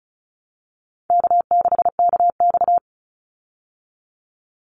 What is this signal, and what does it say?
Morse code, heard as a keyed tone, about 34 wpm, 700 Hz. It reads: K6KX
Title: Easy Call Signs Repeated 3x - 35wpm